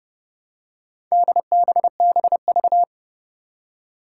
Morse code, heard as a keyed tone, about 29 wpm, 700 Hz. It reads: DBBV
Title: Four Challenging Letter-Number Combinations 30wpm